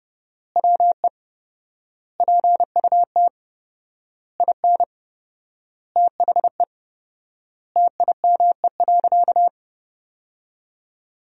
Morse code, heard as a keyed tone, about 30 wpm, 700 Hz. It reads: WE PUT IN THE TIME.
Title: 4x Word-Spacing - Sentences from Top 200 Words - 30wpm